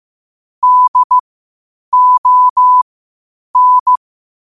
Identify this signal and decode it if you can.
Morse code, heard as a keyed tone, about 15 wpm, spaced as for 8 wpm, 1000 Hz. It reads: DON
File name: qotdfiles / 8wpm